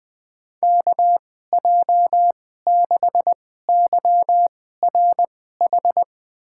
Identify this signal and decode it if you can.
Morse code, heard as a keyed tone, about 21 wpm, spaced as for 15 wpm, 700 Hz. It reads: KJ6YRH
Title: USA Call Signs - 6 Characters - 2x Character Spacing - 20wpm